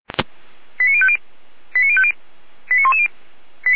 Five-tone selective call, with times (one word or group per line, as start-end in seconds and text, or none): 0.8-1.2 89050
1.7-2.1 89050
2.7-3.1 89110